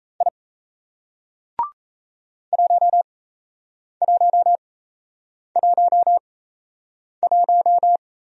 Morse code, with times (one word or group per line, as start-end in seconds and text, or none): 0.2-0.3 I
2.5-8.0 1111